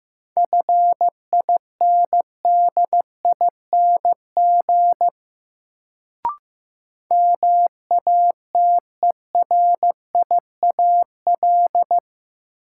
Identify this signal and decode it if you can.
Morse code, heard as a keyed tone, about 15 wpm, 700 Hz. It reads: FINDING MATERIAL
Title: Sentences from Top 1100 Words - New Word Drill Alt - 15wpm